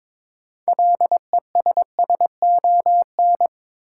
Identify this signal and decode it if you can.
Morse code, heard as a keyed tone, about 22 wpm, 700 Hz. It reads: LESSON
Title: Sentences from Top 1200 Words - New Word Drill - 22wpm